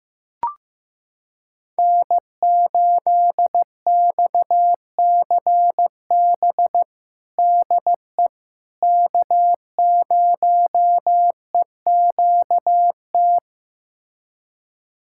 Morse code, heard as a keyed tone, about 15 wpm, 700 Hz. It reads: N8XCB DE K0EQT
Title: USA Callsign DE Callsign - 5 Characters - 15wpm